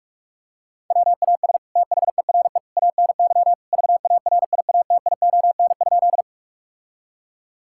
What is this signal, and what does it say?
Morse code, heard as a keyed tone, about 46 wpm, 700 Hz. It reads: WAS THERE ANY VARIATION?
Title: Sentences from Top 1600 Words 45wpm